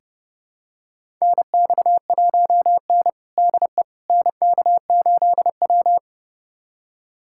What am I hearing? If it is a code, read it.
Morse code, heard as a keyed tone, about 32 wpm, 700 Hz. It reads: NX1N DE NK8W
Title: USA Callsign DE Callsign - 4 Characters - 30wpm